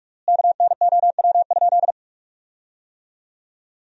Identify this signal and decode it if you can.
Morse code, heard as a keyed tone, about 41 wpm, 700 Hz. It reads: KNOW?
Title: 3x Word-Spacing - Sentences from Top 200 Words - 45wpm